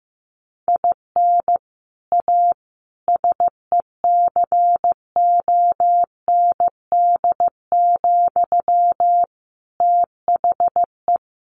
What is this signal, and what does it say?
Morse code, heard as a keyed tone, about 14 wpm, 700 Hz. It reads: IN A SECOND, THE
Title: Sentences from Top 1500 Words 15wpm